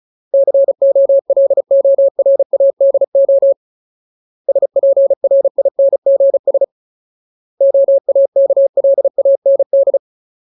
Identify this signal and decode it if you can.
Morse code, heard as a keyed tone, about 34 wpm, 550 Hz. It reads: COLORADO SPRINGS OAKLAND
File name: Ultimate Word List / 35WPM Ultimate